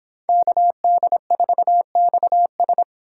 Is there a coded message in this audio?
Yes: Morse code from 0.3 to 2.8 s, heard as a keyed tone, about 26 wpm, 700 Hz. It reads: KD4XS